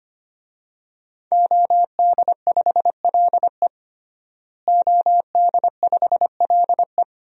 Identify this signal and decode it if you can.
Morse code, heard as a keyed tone, about 25 wpm, 700 Hz. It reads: OD5LE OD5LE
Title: DXCC-100 Call Signs - 5 Characters - Repeated 2x - 25wpm